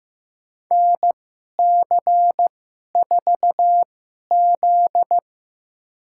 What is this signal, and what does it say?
Morse code, heard as a keyed tone, about 15 wpm, spaced as for 11 wpm, 700 Hz. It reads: NC4Z
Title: USA Call Signs - 2x Character Spacing - 15wpm